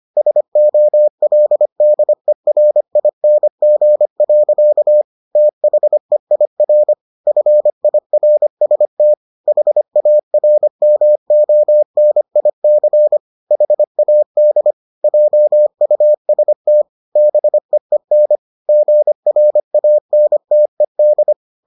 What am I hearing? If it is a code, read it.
Morse code, heard as a keyed tone, about 25 wpm, 600 Hz. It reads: SOLDERING. THEIR FIRST HARMONIC HAD JUST BEEN GRANTED